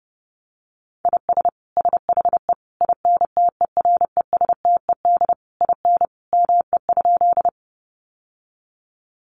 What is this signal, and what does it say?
Morse code, heard as a keyed tone, about 30 wpm, 700 Hz. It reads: IS SHE INTERESTED IN ME?
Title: Sentences from Top 900 Words 30wpm